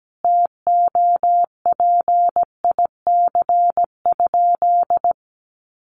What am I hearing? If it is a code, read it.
Morse code, heard as a keyed tone, about 17 wpm, 700 Hz. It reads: TOPIC?
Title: Sentences from Top 1200 Words 17wpm